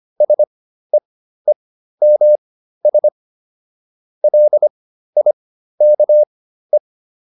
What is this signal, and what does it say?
Morse code, heard as a keyed tone, about 25 wpm, spaced as for 13 wpm, 600 Hz. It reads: SEEMS LIKE